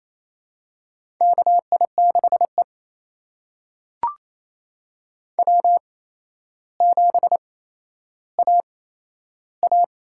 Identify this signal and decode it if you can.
Morse code, heard as a keyed tone, about 28 wpm, spaced as for 12 wpm, 700 Hz. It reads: KI6E W7AA
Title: USA Call Signs - 8x Character Spacing - 28wpm